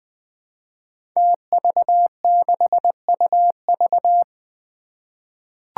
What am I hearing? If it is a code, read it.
Morse code, heard as a keyed tone, about 20 wpm, 700 Hz. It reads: TV6UV
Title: Five Challenging Letter-Number Combinations 20wpm